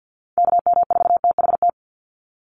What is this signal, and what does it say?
Morse code, heard as a keyed tone, about 51 wpm, 700 Hz. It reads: KM4THT
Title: USA Call Signs 50wpm